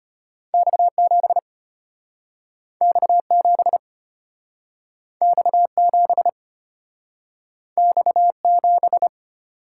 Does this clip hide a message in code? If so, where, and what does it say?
Yes: Morse code from 0.5 to 9.1 s, heard as a keyed tone, about 33 wpm, 700 Hz. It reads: X7 X7 X7 X7